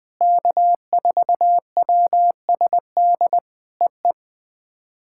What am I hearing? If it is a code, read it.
Morse code, heard as a keyed tone, about 21 wpm, 700 Hz. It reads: K4WSD EE
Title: POTA QSO Practice - 20wpm